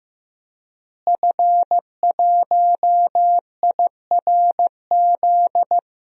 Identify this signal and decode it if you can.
Morse code, heard as a keyed tone, about 15 wpm, 700 Hz. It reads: F1IRZ
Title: DXCC-100 Call Signs - 5 Characters - Repeated 3x - 15wpm